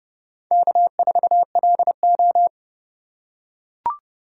Morse code, heard as a keyed tone, about 31 wpm, 700 Hz. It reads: K4LO
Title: USA Call Signs - 4 Characters - 2x Character Spacing - 30wpm